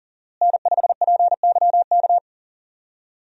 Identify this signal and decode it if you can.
Morse code, heard as a keyed tone, about 39 wpm, 700 Hz. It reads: N5PYK
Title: USA Call Signs - 5 Characters - Repeated 2x - 40wpm